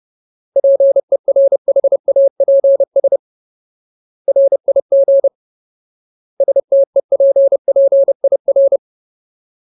Morse code, heard as a keyed tone, about 30 wpm, 550 Hz. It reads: PERHAPS RIG STEPPIR
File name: Ultimate Word List / 30WPM Ultimate